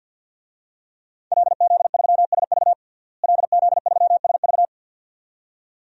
Morse code, heard as a keyed tone, about 50 wpm, 700 Hz. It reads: LZ3SV LZ3SV
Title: DXCC-100 Call Signs - 5 Characters - Repeated 2x - 50wpm